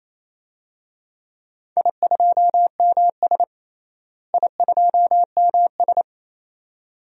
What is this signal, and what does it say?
Morse code, heard as a keyed tone, about 28 wpm, 700 Hz. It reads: I2MS I2MS
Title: DXCC-100 Call Signs - 4 Characters - Repeated 2x - 28wpm